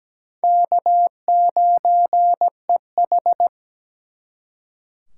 Morse code, heard as a keyed tone, about 20 wpm, 700 Hz. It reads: K9EH
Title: Speed Racing - USA 1x2 Call Signs 17wpm